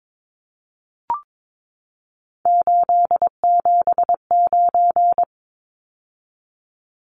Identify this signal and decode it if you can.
Morse code, heard as a keyed tone, about 22 wpm, 700 Hz. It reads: 879
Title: Numbers - 3-Digit 22wpm